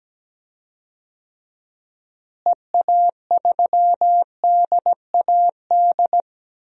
Morse code, heard as a keyed tone, about 17 wpm, 700 Hz. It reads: EA3DAD
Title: DXCC-100 Call Signs - 6 Characters - Repeated 3x - 17wpm